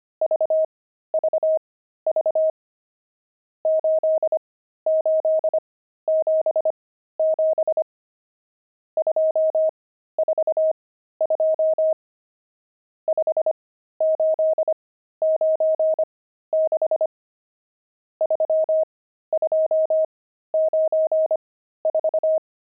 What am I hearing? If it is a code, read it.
Morse code, heard as a keyed tone, about 25 wpm, spaced as for 13 wpm, 630 Hz. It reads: VVV 8877 242 5896 3294
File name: pangram-fs-13-cs-25-nums19